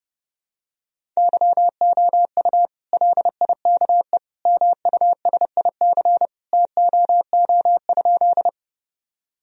Morse code, heard as a keyed tone, about 30 wpm, 700 Hz. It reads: YOU LIKE MUSIC TOO?
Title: Sentences from Top 1000 Words - Review Part 1 - 30wpm